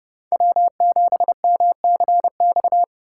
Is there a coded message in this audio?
Yes: Morse code from 0.3 to 2.8 s, heard as a keyed tone, about 29 wpm, 700 Hz. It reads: W7MCX